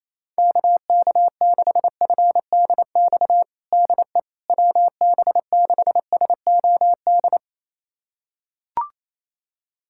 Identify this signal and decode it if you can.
Morse code, heard as a keyed tone, about 28 wpm, 700 Hz. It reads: KK6FDX DE WB6SOD
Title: USA Callsign DE Callsign - 6 Characters - 28wpm